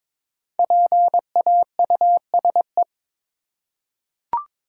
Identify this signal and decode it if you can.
Morse code, heard as a keyed tone, about 22 wpm, 690 Hz. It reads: PAUSE
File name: 5 Letter Words - First 3 Letters Given - 22wpm